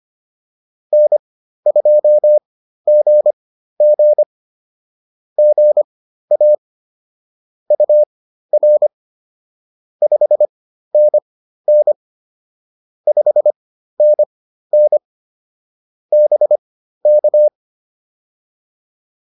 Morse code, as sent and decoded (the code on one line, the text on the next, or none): -. ..--- --. --. / --. .- / ..- .-. / ..... -. -. / ..... -. -. / -... -.-
N2GG GA UR 5NN 5NN BK